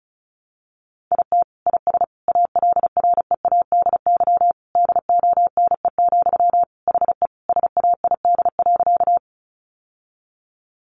Morse code, heard as a keyed tone, about 34 wpm, 700 Hz. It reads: IT IS ALREADY DONE, HE SAID.
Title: Sentences from Top 700 Words 35wpm